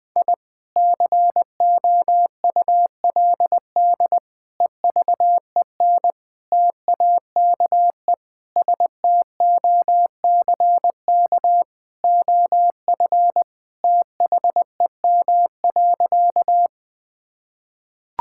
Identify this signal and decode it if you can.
Morse code, heard as a keyed tone, about 20 wpm, 700 Hz. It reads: I COULD EVEN TAKE STOCK OF THEM.
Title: Sentences from Top 1000 Words - Review Part 5 - 20wpm